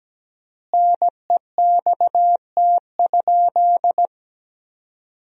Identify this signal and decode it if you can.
Morse code, heard as a keyed tone, about 17 wpm, 700 Hz. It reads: NEXT?
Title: 4x Word-Spacing - Sentences from Top 300 Words - 17wpm